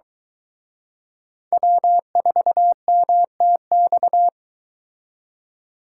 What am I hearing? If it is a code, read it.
Morse code, heard as a keyed tone, about 22 wpm, 700 Hz. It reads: W4MTX